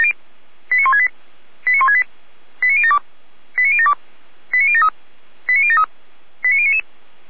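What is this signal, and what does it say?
Five-tone selective call (ZVEI), heard as quick runs of five tones, tones of about 80 ms. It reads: 89168 89168 89072 89072 89073 89073 89099
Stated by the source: ZVEI five-tone call received by SDR.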